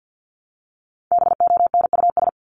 Morse code, heard as a keyed tone, about 50 wpm, 700 Hz. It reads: BONUS
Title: Top 1751-2000 Words 50wpm